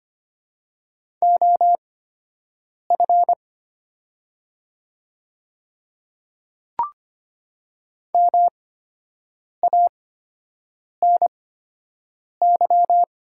Morse code, heard as a keyed tone, about 25 wpm, spaced as for 7 wpm, 700 Hz. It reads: OF MANY
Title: 8x Character Spacing - Top 100 Words - 25wpm